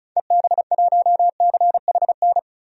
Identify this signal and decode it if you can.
Morse code, heard as a keyed tone, about 36 wpm, 700 Hz. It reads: EB1CHN
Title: DXCC-100 Call Signs - 6 Characters - 35wpm